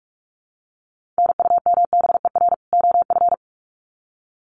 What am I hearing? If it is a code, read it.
Morse code, heard as a keyed tone, about 44 wpm, 700 Hz. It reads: NUMBER OF
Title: Sets of 2 Words 45wpm